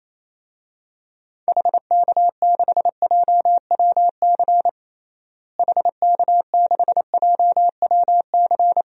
Morse code, heard as a keyed tone, about 28 wpm, 700 Hz. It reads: HK6JWC HK6JWC
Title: DXCC-100 Call Signs - 6 Characters - Repeated 3x - 28wpm